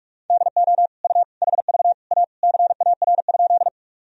Morse code, heard as a keyed tone, about 46 wpm, 700 Hz. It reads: DO U HV A CAR?
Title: Sentences from Top 300 Words and Abbreviations 45wpm